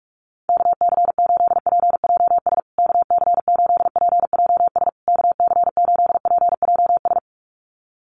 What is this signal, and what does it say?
Morse code, heard as a keyed tone, about 44 wpm, 700 Hz. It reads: KC8PJS KC8PJS KC8PJS
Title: USA Call Signs Repeated 3x 45wpm